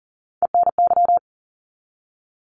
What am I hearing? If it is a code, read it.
Morse code, heard as a keyed tone, about 40 wpm, 700 Hz. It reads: ENY
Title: ARRL Sections 40wpm